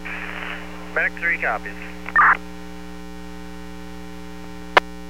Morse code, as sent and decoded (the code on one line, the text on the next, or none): none